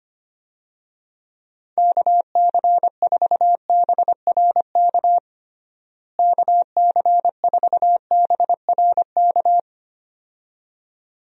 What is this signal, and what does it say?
Morse code, heard as a keyed tone, about 25 wpm, 700 Hz. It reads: KC4BRK KC4BRK
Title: USA Call Signs - 6 Characters - Repeated 2x - 25wpm